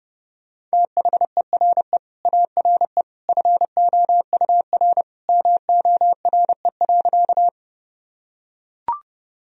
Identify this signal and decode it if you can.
Morse code, heard as a keyed tone, about 30 wpm, 700 Hz. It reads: THERE ARE FOUR MORE.
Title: Sentences from Top 1000 Words - Review Part 1 - 30wpm